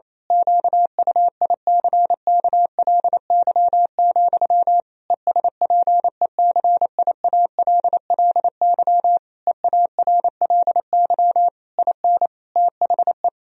Morse code, heard as a keyed tone, about 27 wpm, 700 Hz. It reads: QUICKLY, ESPECIALLY EARLY IN THE
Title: Sentences from Top 1000 Words - Review Part 4 - 28wpm